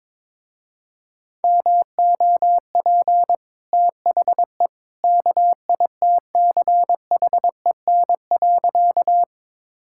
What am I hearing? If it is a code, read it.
Morse code, heard as a keyed tone, about 22 wpm, 700 Hz. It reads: MOP THE KITCHEN.
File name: practice / 22wpm / Easy Sentences with Hints - 22wpm